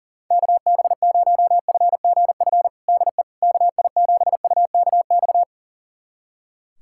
Morse code, heard as a keyed tone, about 40 wpm, 700 Hz. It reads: KB0FGF DE KI7UKX
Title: USA Callsign DE Callsign - 6 Characters - 40wpm